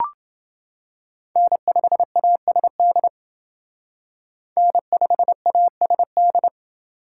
Morse code, heard as a keyed tone, about 27 wpm, 700 Hz. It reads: N5ASD N5ASD